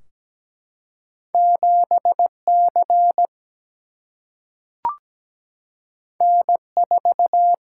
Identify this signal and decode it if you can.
Morse code, heard as a keyed tone, about 17 wpm, 700 Hz. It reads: 7C N4
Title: Single-Letter Single-Number 17wpm